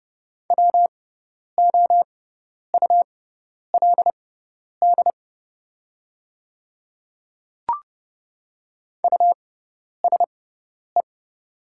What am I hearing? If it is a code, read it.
Morse code, heard as a keyed tone, about 29 wpm, spaced as for 10 wpm, 700 Hz. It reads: WOULD USE
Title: 6x Character Spacing - Top 300 Words - 30wpm